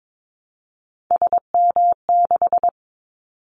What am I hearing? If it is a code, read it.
Morse code, heard as a keyed tone, about 22 wpm, 700 Hz. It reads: SM6
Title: Callsign Prefixes for Top 25 DXCC - Repeated 3x - 22wpm